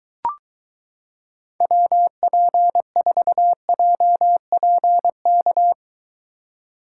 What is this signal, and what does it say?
Morse code, heard as a keyed tone, about 23 wpm, 700 Hz. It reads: WP4JPK